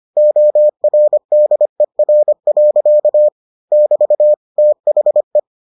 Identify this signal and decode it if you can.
Morse code, heard as a keyed tone, about 25 wpm, 600 Hz. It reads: ORDER. <BT> THE